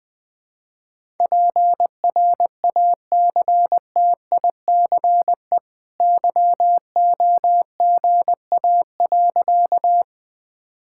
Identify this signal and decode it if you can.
Morse code, heard as a keyed tone, about 20 wpm, 700 Hz. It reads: PRACTICE YOGA.
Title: Easy Sentences with Hints - 20wpm